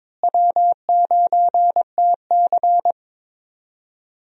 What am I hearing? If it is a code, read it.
Morse code, heard as a keyed tone, about 22 wpm, 700 Hz. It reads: W9TC
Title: USA Call Signs - 4 Characters - Repeated 2x - 22wpm